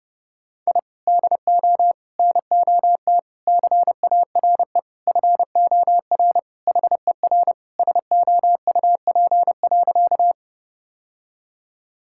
Morse code, heard as a keyed tone, about 30 wpm, 700 Hz. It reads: I DO NOT CARE FOR HER SOUP.
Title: Sentences from Top 1000 Words - Review Part 6 - 30wpm